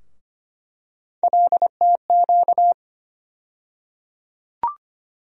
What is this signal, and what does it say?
Morse code, heard as a keyed tone, about 25 wpm, 700 Hz. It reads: LTQ
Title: USA Call Sign Postfixes - 3 Letters 25wpm